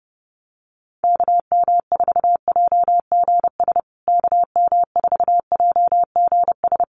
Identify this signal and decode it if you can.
Morse code, heard as a keyed tone, about 30 wpm, 700 Hz. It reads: KM4JGS KM4JGS
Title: USA Call Signs Repeated 3x 30wpm